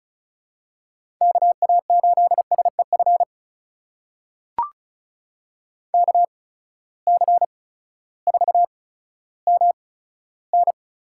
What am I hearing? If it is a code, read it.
Morse code, heard as a keyed tone, about 36 wpm, spaced as for 16 wpm, 700 Hz. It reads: KA8SEF KC4MN
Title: USA Call Signs - 8x Character Spacing - 35wpm